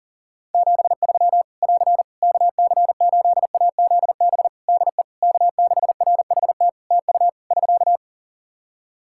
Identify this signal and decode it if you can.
Morse code, heard as a keyed tone, about 42 wpm, 700 Hz. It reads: 73 <AR> KC8AZB DE K6RHT TU <SK>
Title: Practice QSO - Exchange 4 - Ending - 40wpm